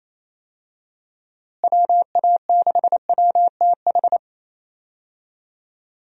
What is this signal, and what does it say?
Morse code, heard as a keyed tone, about 28 wpm, 700 Hz. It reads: WA6WTH
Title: USA Call Signs - 6 Characters - Repeated 3x - 28wpm